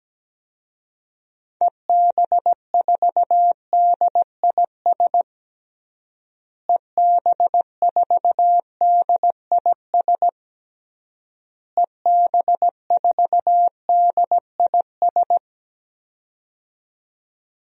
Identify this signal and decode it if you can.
Morse code, heard as a keyed tone, about 17 wpm, 700 Hz. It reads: EB4DIS EB4DIS EB4DIS
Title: DXCC-100 Call Signs - 6 Characters - Repeated 3x - 17wpm